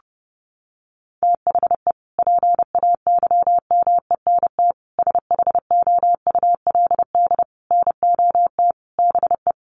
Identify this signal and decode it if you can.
Morse code, heard as a keyed tone, about 30 wpm, 700 Hz. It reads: THE PAYMENT SHOULD NOT BE
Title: Sentences from Top 1200 Words 30wpm